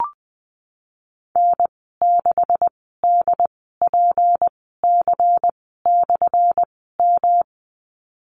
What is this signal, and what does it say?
Morse code, heard as a keyed tone, about 20 wpm, spaced as for 15 wpm, 700 Hz. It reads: N6DPC/M
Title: 179 Lesson 29g - Slash - Call Signs 20wpm